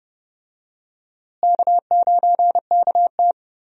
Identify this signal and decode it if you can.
Morse code, heard as a keyed tone, about 30 wpm, 700 Hz. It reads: K9KT
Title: USA Call Signs - 4 Characters - Repeated 2x - 30wpm